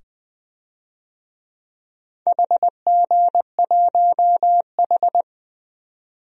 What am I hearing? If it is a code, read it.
Morse code, heard as a keyed tone, about 20 wpm, 700 Hz. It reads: HG1H
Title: DXCC-100 Call Signs - 4 Characters - 20wpm